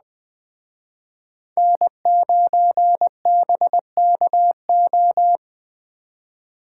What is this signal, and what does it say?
Morse code, heard as a keyed tone, about 20 wpm, 700 Hz. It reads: N9BKO